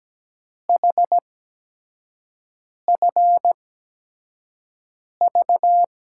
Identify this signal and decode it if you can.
Morse code, heard as a keyed tone, about 17 wpm, spaced as for 5 wpm, 700 Hz. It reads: HFV
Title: USA Call Signs - 6 Characters - 8x Character Spacing - 17wpm